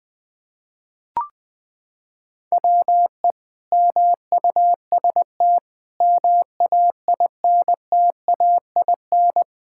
Morse code, heard as a keyed tone, about 20 wpm, 700 Hz. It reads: WE MUST MAINTAIN
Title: Sentences from Top 1000 Words - Review Part 5 - 20wpm